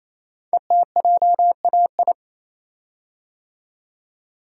Morse code, heard as a keyed tone, about 29 wpm, 700 Hz. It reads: ETJAI
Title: Code Groups - 5 Letters - 28wpm